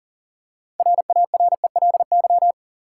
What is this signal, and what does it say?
Morse code, heard as a keyed tone, about 40 wpm, 700 Hz. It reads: RARELY
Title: Six Letter Words - 40wpm